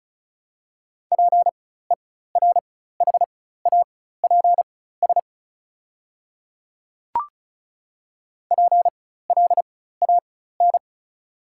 Morse code, heard as a keyed tone, about 36 wpm, spaced as for 16 wpm, 690 Hz. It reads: PERHAPS PLAN
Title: 4x Character Spacing - Top 500 Words - 35wpm